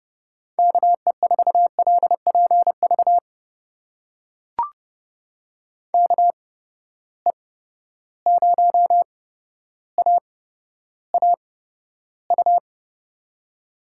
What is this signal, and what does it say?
Morse code, heard as a keyed tone, about 32 wpm, spaced as for 13 wpm, 700 Hz. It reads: KE4LPV KE0AAU
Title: USA Call Signs - 6 Characters - 8x Character Spacing - 30wpm